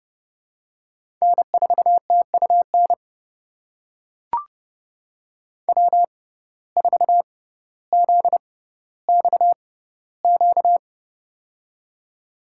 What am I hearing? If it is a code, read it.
Morse code, heard as a keyed tone, about 30 wpm, spaced as for 15 wpm, 700 Hz. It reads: N4TUN W4ZXQ
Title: USA Call Signs - 6x Character Spacing - 30wpm